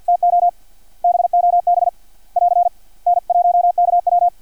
Morse code, heard as a keyed tone, about 50 wpm, 700 Hz. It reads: TO BOB <AR> N1KW
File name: archived / CRSnet